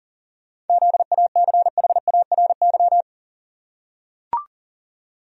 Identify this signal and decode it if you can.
Morse code, heard as a keyed tone, about 40 wpm, 700 Hz. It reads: ZACHARY
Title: USA Common Names - 40wpm